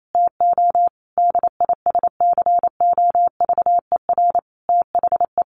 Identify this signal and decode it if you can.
Morse code, heard as a keyed tone, about 28 wpm, 700 Hz. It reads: TO DISCOVER THE
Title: Sentences from Top 1100 Words 28wpm